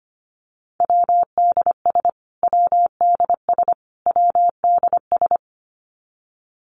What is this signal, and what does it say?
Morse code, heard as a keyed tone, about 25 wpm, 700 Hz. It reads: WDS WDS WDS
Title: Abbreviations Repeated 3x - 25wpm